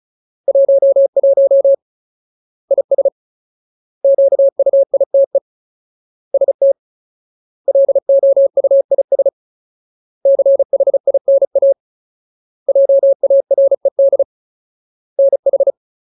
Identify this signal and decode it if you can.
Morse code, heard as a keyed tone, about 35 wpm, 550 Hz. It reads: 11 IS QUITE ST LOUIS CHINA JARED NH
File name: Ultimate Word List / 35WPM Ultimate